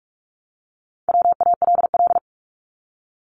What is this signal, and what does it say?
Morse code, heard as a keyed tone, about 46 wpm, 700 Hz. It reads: WALL
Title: Multi-letter Phonemes Repeated 3x 45wpm